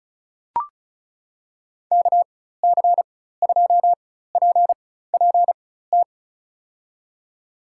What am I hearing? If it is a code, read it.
Morse code, heard as a keyed tone, about 34 wpm, spaced as for 16 wpm, 700 Hz. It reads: KC2PPT
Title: USA Call Signs - 6 Characters - 4x Character Spacing - 35wpm